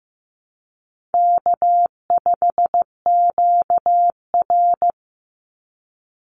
Morse code, heard as a keyed tone, about 15 wpm, 700 Hz. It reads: K5QR
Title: NA SOTA Chaser Callsigns 15wpm